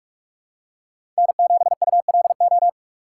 Morse code, heard as a keyed tone, about 41 wpm, 700 Hz. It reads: N7ULO
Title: USA Call Signs - 5 Characters - 4x Character Spacing - 45wpm